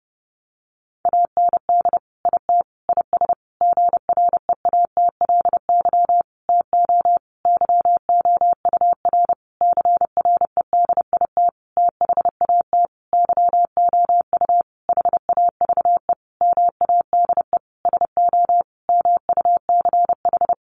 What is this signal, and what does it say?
Morse code, heard as a keyed tone, about 30 wpm, 700 Hz. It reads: AND IT IS GREATLY TO YOUR CREDIT THAT YOU HAVE MADE SO MUCH
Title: Sentences from Top 1400 Words 30wpm